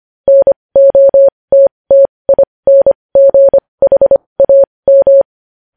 Morse code, heard as a keyed tone, about 25 wpm, spaced as for 20 wpm, 550 Hz. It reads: NOTTINGHAM
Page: news-headlines-in-morse-code-at-20-wpm